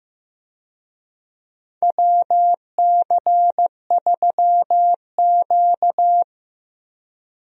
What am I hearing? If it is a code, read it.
Morse code, heard as a keyed tone, about 15 wpm, 700 Hz. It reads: WC3Q
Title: USA Call Signs - 4 Characters - Repeated 2x - 15wpm